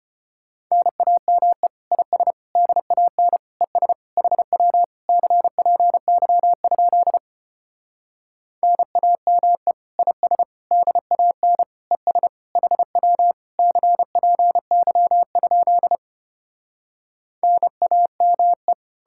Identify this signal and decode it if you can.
Morse code, heard as a keyed tone, about 33 wpm, 700 Hz. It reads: NAME IS DAN ES HW CPY? NAME IS DAN ES HW CPY? NAME